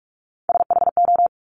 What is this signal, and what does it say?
Morse code, heard as a keyed tone, about 46 wpm, 700 Hz. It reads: SHO
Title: Top Three Letter Combinations in Words 45wpm